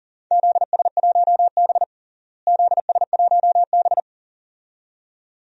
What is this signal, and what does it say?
Morse code, heard as a keyed tone, about 40 wpm, 700 Hz. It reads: ZS1B ZS1B
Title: DXCC-100 Call Signs - 4 Characters - Repeated 2x - 40wpm